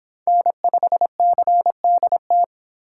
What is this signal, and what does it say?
Morse code, heard as a keyed tone, about 25 wpm, 700 Hz. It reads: N5CDT